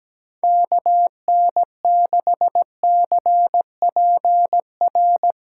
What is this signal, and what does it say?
Morse code, heard as a keyed tone, about 17 wpm, 700 Hz. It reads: KN6CPR
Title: USA Callsign DE Callsign 17wpm